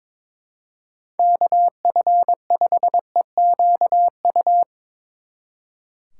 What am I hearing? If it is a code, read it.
Morse code, heard as a keyed tone, about 23 wpm, 700 Hz. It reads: KF5EQU
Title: Speed Racing - USA 2x3 Call Signs 22wpm